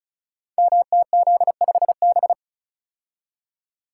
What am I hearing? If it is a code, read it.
Morse code, heard as a keyed tone, about 34 wpm, 700 Hz. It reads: MTZ5B
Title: Code Groups - 5 Letter:Numbers - 35wpm